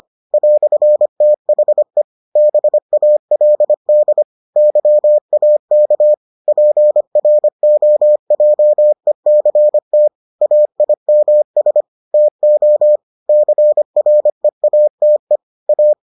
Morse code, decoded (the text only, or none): "THE BALD YAK PROJECT AIMS TO CREATE A